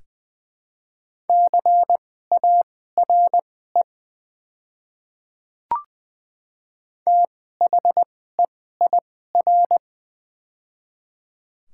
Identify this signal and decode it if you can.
Morse code, heard as a keyed tone, about 20 wpm, spaced as for 14 wpm, 700 Hz. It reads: CARE THEIR
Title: 2x Character Spacing - Top 500 Words - 20wpm